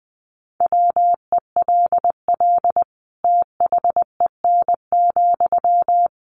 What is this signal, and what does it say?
Morse code, heard as a keyed tone, about 20 wpm, 700 Hz. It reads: WELL THEN,
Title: Sentences from Top 200 Words 20wpm